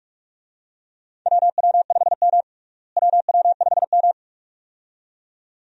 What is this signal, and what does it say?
Morse code, heard as a keyed tone, about 46 wpm, 700 Hz. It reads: WW5M WW5M
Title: USA Call Signs - 4 Characters - Repeated 2x - 45wpm